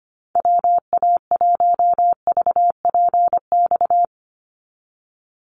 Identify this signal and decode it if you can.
Morse code, heard as a keyed tone, about 25 wpm, 700 Hz. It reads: WA1VPX
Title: USA Call Signs 25wpm